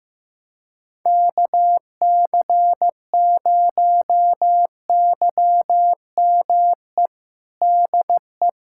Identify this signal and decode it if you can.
Morse code, heard as a keyed tone, about 15 wpm, 700 Hz. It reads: KC0YME DE
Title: USA Callsign DE Callsign - 6 Characters - 15wpm